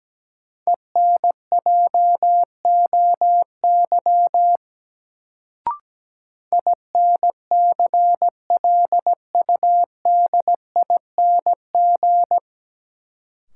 Morse code, heard as a keyed tone, about 17 wpm, 700 Hz. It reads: ENJOY INCLUDING
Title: Sentences from Top 1000 Words - Review Part 4 - New Word Drill - 17wpm